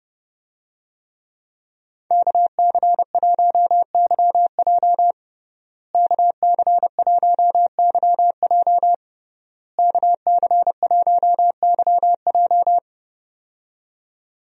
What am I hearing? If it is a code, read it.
Morse code, heard as a keyed tone, about 30 wpm, 700 Hz. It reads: KC1YJ KC1YJ KC1YJ
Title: USA Call Signs - 5 Characters - Repeated 3x - 30wpm